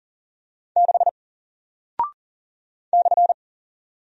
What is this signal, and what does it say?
Morse code, heard as a keyed tone, about 40 wpm, 700 Hz. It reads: 6 /
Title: Single Character (Letter:Number:Punctuation:Prosign) 40wpm